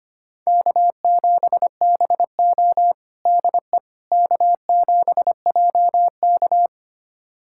Morse code, heard as a keyed tone, about 25 wpm, 700 Hz. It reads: K7BO DE K7JK
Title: USA Callsign DE Callsign - 4 Characters - 25wpm